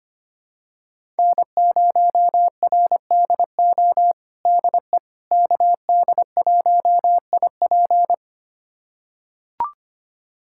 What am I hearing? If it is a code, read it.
Morse code, heard as a keyed tone, about 25 wpm, 700 Hz. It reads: N0RDO DE KD1IP
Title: USA Callsign DE Callsign - 5 Characters - 25wpm